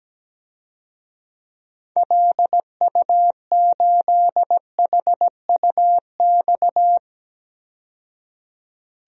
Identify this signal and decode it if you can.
Morse code, heard as a keyed tone, about 17 wpm, 700 Hz. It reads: LU8HUX
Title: DXCC-100 Call Signs - 6 Characters - 17wpm